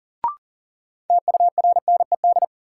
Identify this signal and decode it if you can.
Morse code, heard as a keyed tone, about 40 wpm, 700 Hz. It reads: TURNED
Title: Sets of 4 Words - New Word Drill - 40wpm